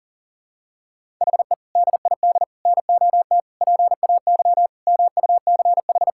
Morse code, heard as a keyed tone, about 40 wpm, 700 Hz. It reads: HE DID NOT PAY MUCH
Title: Sentences from Top 1000 Words - Review Part 6 - 40wpm